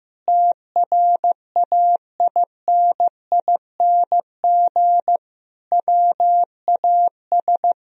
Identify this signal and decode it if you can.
Morse code, heard as a keyed tone, about 15 wpm, 700 Hz. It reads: TRAINING WAS
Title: Sentences from Top 1000 Words - Review Part 4 - 15wpm